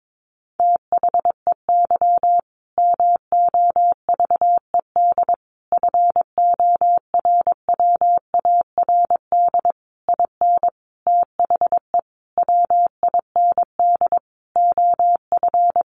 Morse code, heard as a keyed tone, about 22 wpm, 700 Hz. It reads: THEY MOVED FORWARD IN THE WIND OF
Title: A Sound Of Thunder 22wpm